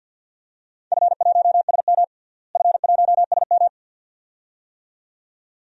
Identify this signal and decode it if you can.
Morse code, heard as a keyed tone, about 53 wpm, 700 Hz. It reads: F1SM F1SM
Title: DXCC-100 Call Signs - 4 Characters - Repeated 2x - 50wpm